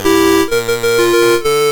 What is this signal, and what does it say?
There is no code.